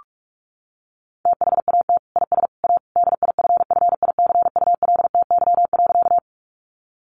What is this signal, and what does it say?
Morse code, heard as a keyed tone, about 44 wpm, 700 Hz. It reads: THAT IS A DIFFICULTY.
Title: Sentences from Top 1500 Words 45wpm